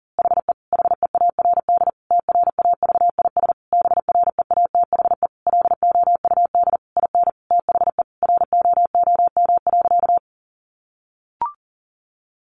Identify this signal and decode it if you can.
Morse code, heard as a keyed tone, about 40 wpm, 700 Hz. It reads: HE HEARD TRAVIS BREATHE LOUD IN THE ROOM.
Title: A Sound Of Thunder 40wpm